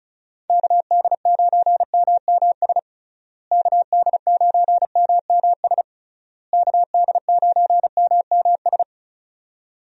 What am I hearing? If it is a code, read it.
Morse code, heard as a keyed tone, about 36 wpm, 700 Hz. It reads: KD9MMS KD9MMS KD9MMS
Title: USA Call Signs - 6 Characters - Repeated 3x - 35wpm